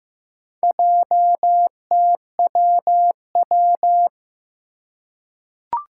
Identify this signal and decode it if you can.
Morse code, heard as a keyed tone, about 16 wpm, 700 Hz. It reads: JTWW
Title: Four Challenging Letter-Number Combinations 15wpm